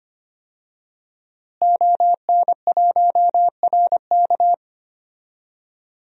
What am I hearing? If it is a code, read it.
Morse code, heard as a keyed tone, about 25 wpm, 700 Hz. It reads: ON1RK
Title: DXCC-100 Call Signs - 5 Characters - Repeated 3x - 25wpm